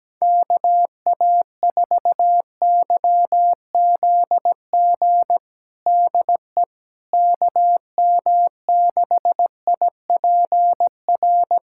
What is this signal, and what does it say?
Morse code, heard as a keyed tone, about 17 wpm, 700 Hz. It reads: KA4YZG DE KM6IPR
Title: USA Callsign DE Callsign - 6 Characters - 17wpm